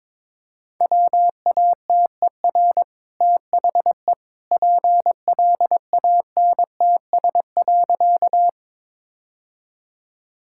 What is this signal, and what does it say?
Morse code, heard as a keyed tone, about 22 wpm, 690 Hz. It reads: WATER THE PLANTS.
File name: practice / 22wpm / Easy Sentences with Hints - 22wpm